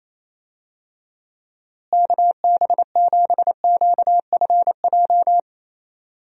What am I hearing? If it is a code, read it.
Morse code, heard as a keyed tone, about 28 wpm, 700 Hz. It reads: KB7QFJ
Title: USA Call Signs - 6 Characters - Repeated 2x - 28wpm